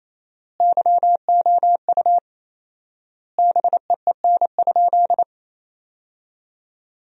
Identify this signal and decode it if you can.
Morse code, heard as a keyed tone, about 28 wpm, 700 Hz. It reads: YOU BEEN?
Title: 4x Word-Spacing - Sentences from Top 400 Words - 28wpm